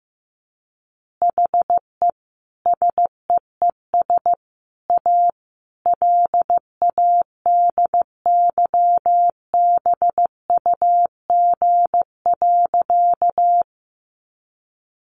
Morse code, heard as a keyed tone, about 14 wpm, 700 Hz. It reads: HE SEES A LADYBUG.
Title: Additional Easy Phrases 15wpm